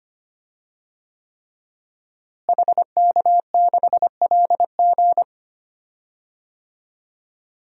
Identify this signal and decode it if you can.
Morse code, heard as a keyed tone, about 25 wpm, 700 Hz. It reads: HK6LG
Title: DXCC-100 Call Signs - 5 Characters - Repeated 3x - 25wpm